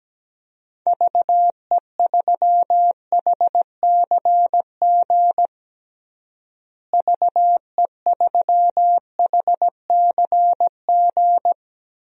Morse code, heard as a keyed tone, about 17 wpm, 700 Hz. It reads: VE3HCG VE3HCG
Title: DXCC-100 Call Signs - 6 Characters - Repeated 3x - 17wpm